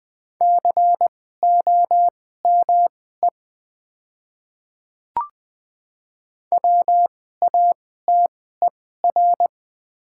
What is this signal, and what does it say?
Morse code, heard as a keyed tone, about 20 wpm, spaced as for 14 wpm, 700 Hz. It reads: COME WATER
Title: 2x Character Spacing - Top 100 Words - 20wpm